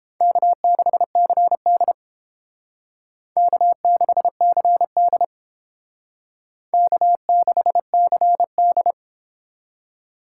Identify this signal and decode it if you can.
Morse code, heard as a keyed tone, about 30 wpm, 700 Hz. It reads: K6CD K6CD K6CD